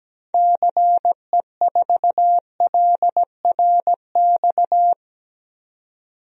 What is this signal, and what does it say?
Morse code, heard as a keyed tone, about 17 wpm, 700 Hz. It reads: CE4LRX
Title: DXCC-100 Call Signs - 6 Characters - 17wpm